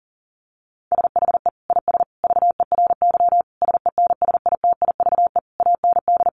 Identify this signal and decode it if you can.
Morse code, heard as a keyed tone, about 37 wpm, 700 Hz. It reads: SHE IS VERY SENSITIVE AND
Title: Sentences from Top 1400 Words 40wpm